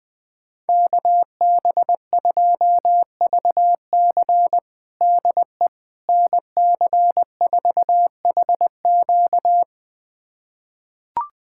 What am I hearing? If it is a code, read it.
Morse code, heard as a keyed tone, about 20 wpm, 700 Hz. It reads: KB2VC DE NC4HQ
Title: USA Callsign DE Callsign - 5 Characters - 20wpm